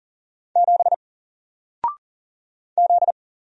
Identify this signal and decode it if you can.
Morse code, heard as a keyed tone, about 40 wpm, 700 Hz. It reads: 7 Z
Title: Single Character (Letter:Number:Punctuation:Prosign) 40wpm